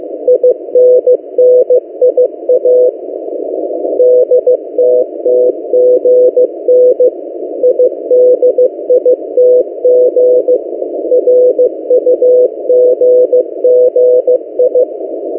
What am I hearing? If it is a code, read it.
Morse code, heard as a keyed tone, about 15 wpm, 500 Hz. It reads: INNIA DTTGN IDITG RUGGI
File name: M08a- -1355utc-8096khz